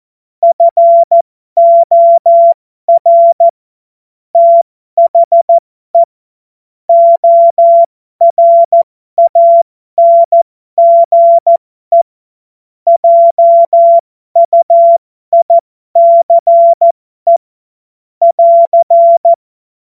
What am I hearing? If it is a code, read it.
Morse code, heard as a keyed tone, about 14 wpm, spaced as for 12 wpm, 680 Hz. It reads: FOR THE ORANGE JUICE <AR>
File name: Morsekurs / Witze